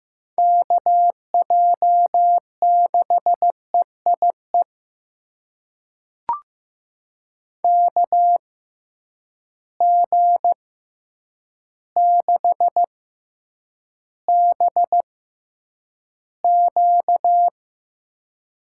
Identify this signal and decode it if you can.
Morse code, heard as a keyed tone, about 15 wpm, spaced as for 8 wpm, 700 Hz. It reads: KJ6EIE KG6BQ
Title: USA Call Signs - 6x Character Spacing - 15wpm